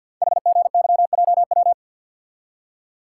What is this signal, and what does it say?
Morse code, heard as a keyed tone, about 48 wpm, 700 Hz. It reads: HGYJW
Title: Code Groups - 5 Letters - 50wpm